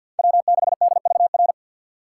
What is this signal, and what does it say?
Morse code, heard as a keyed tone, about 53 wpm, 700 Hz. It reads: W6DVR
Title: USA Call Signs - 5 Characters - 6x Character Spacing - 50wpm